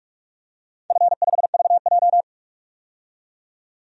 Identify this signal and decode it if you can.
Morse code, heard as a keyed tone, about 44 wpm, 690 Hz. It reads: F5VJ
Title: DXCC-100 Call Signs - 4 Characters - 45wpm